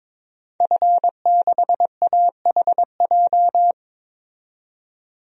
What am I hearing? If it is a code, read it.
Morse code, heard as a keyed tone, about 22 wpm, 700 Hz. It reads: F6AHJ
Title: DXCC-100 Call Signs - 5 Characters - 22wpm